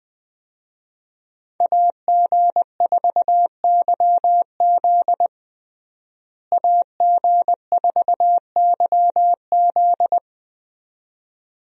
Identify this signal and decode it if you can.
Morse code, heard as a keyed tone, about 20 wpm, 700 Hz. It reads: AG4YZ AG4YZ
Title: USA Call Signs - 5 Characters - Repeated 2x - 20wpm